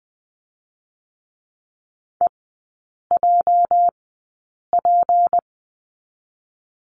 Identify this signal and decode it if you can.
Morse code, heard as a keyed tone, about 20 wpm, 700 Hz. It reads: E J P
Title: 205 Lesson 33e - Letter j - Three Characters 20wpm